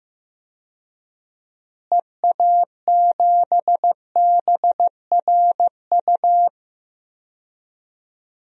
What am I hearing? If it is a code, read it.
Morse code, heard as a keyed tone, about 15 wpm, 700 Hz. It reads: EA7BRU
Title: DXCC-100 Call Signs - 6 Characters - Repeated 3x - 15wpm